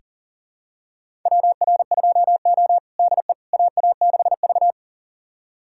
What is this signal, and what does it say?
Morse code, heard as a keyed tone, about 40 wpm, 700 Hz. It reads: WR2O DE AA6V
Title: USA Callsign DE Callsign - 4 Characters - 40wpm